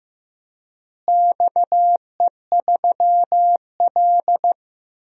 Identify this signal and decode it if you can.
Morse code, heard as a keyed tone, about 15 wpm, 700 Hz. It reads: XE3L
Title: DXCC-100 Call Signs - 4 Characters - Repeated 3x - 15wpm